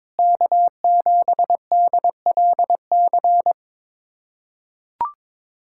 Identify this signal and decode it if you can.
Morse code, heard as a keyed tone, about 22 wpm, 700 Hz. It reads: K7DLC
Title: USA Callsign DE Callsign - 5 Characters - 22wpm